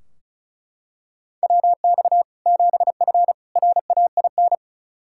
Morse code, heard as a keyed tone, about 34 wpm, 700 Hz. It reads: WX 7F RAIN
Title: Practice QSO - Exchange 2 - Info - 35wpm